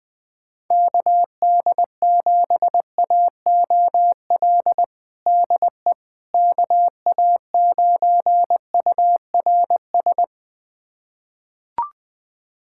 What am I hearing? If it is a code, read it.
Morse code, heard as a keyed tone, about 20 wpm, 700 Hz. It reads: KD7AOL DE KA9URS
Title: USA Callsign DE Callsign 20wpm